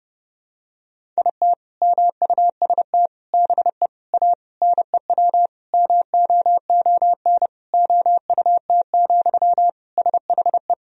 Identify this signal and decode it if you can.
Morse code, heard as a keyed tone, about 30 wpm, 700 Hz. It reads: IT MUST BE A NEW MOON OUT, SHE
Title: Sentences from Top 500 Words and Abbreviations 30wpm